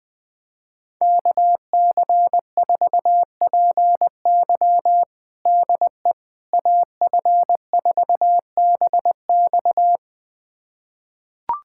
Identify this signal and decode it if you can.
Morse code, heard as a keyed tone, about 20 wpm, 700 Hz. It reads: KC4PY DE AF4BX
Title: USA Callsign DE Callsign - 5 Characters - 20wpm